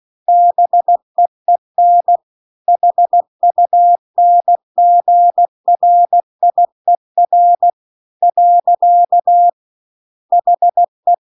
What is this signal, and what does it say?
Morse code, heard as a keyed tone, about 16 wpm, 700 Hz. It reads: BEEN HUNGRIER . HE